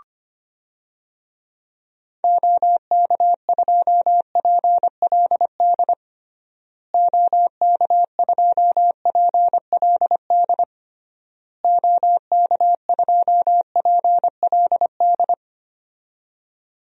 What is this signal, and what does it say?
Morse code, heard as a keyed tone, about 25 wpm, 700 Hz. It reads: OK2PLD OK2PLD OK2PLD
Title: DXCC-100 Call Signs - 6 Characters - Repeated 3x - 25wpm